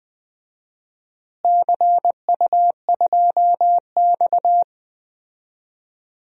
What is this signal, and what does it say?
Morse code, heard as a keyed tone, about 20 wpm, 700 Hz. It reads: CU2X
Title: DXCC-100 Call Signs - 4 Characters - Repeated 3x - 20wpm